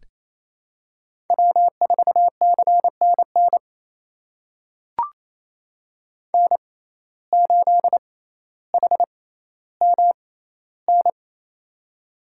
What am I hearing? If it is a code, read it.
Morse code, heard as a keyed tone, about 28 wpm, spaced as for 14 wpm, 700 Hz. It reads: W4CNN N8HMN
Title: USA Call Signs - 5 Characters - 6x Character Spacing - 28wpm